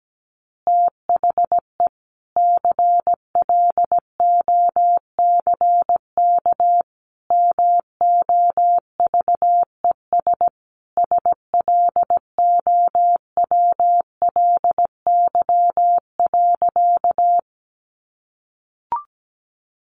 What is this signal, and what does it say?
Morse code, heard as a keyed tone, about 17 wpm, 700 Hz. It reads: THE CLOCK MOVES SLOWLY.
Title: Easy Phrases 17wpm